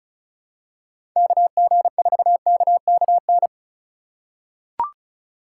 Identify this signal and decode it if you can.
Morse code, heard as a keyed tone, about 34 wpm, 700 Hz. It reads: KG4KKN
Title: USA Call Signs - 6 Characters - 2x Character Spacing - 35wpm